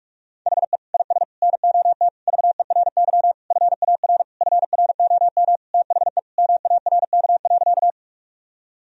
Morse code, heard as a keyed tone, about 41 wpm, 700 Hz. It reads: HE IS NOT VERY FAR FROM THE MARK.
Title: Sentences from Top 1000 Words - Review Part 1 - 45wpm